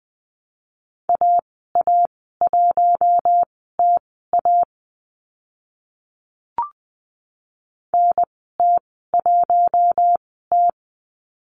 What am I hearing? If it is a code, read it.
Morse code, heard as a keyed tone, about 20 wpm, spaced as for 14 wpm, 700 Hz. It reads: AA1TA NT1T
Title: 042 Lesson 8g - Number 1 - Call Signs 20wpm